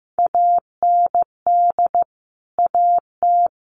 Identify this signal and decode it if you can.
Morse code, heard as a keyed tone, about 15 wpm, 700 Hz. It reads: AND AT
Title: Sentences from Top 1300 Words 15wpm